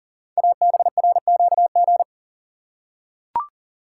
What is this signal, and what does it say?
Morse code, heard as a keyed tone, about 41 wpm, 690 Hz. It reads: ABRQG
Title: Code Groups - 5 Letters - 40wpm